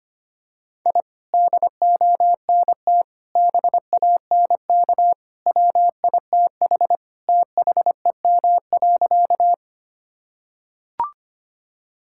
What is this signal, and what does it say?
Morse code, heard as a keyed tone, about 25 wpm, 700 Hz. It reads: I DONT BANK WITH THEM.
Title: Sentences from Top 1000 Words - Review Part 6 - 25wpm